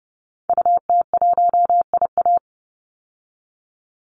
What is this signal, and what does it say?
Morse code, heard as a keyed tone, about 29 wpm, 700 Hz. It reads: UT1IA
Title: Call Signs Repeated 3x - 30wpm